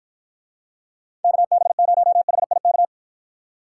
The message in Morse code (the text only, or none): KB0HIK